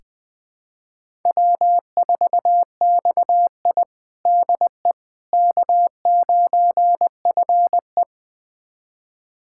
Morse code, heard as a keyed tone, about 20 wpm, 700 Hz. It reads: W4XI DE K9FE
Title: USA Callsign DE Callsign - 4 Characters - 20wpm